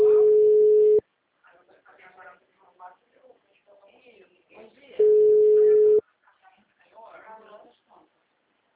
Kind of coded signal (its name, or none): telephone call progress tone